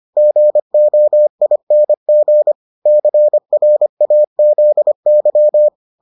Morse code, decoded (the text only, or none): GOING CRAZY